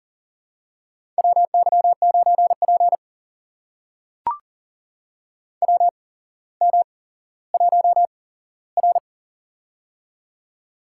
Morse code, heard as a keyed tone, about 39 wpm, spaced as for 17 wpm, 700 Hz. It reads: WY9P WM1R
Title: USA Call Signs - 4 Characters - 8x Character Spacing - 40wpm